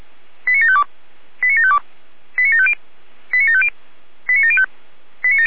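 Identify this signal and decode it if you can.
Five-tone selective call (ZVEI), heard as quick runs of five tones, tones of about 70 ms. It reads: 89752 89752 89750 89750 89705
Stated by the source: ZVEI five-tone selective call received with an SDR.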